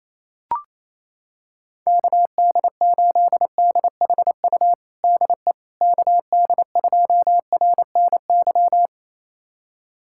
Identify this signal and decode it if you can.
Morse code, heard as a keyed tone, about 28 wpm, 700 Hz. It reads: KD8DHU DE KD2RNY
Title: USA Callsign DE Callsign 28wpm